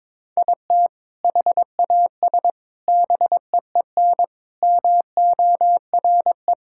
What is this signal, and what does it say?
Morse code, heard as a keyed tone, about 22 wpm, 700 Hz. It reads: IT HAS BEEN MORE
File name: practice / 22wpm / Sentences with 4-Word Phrases - 22wpm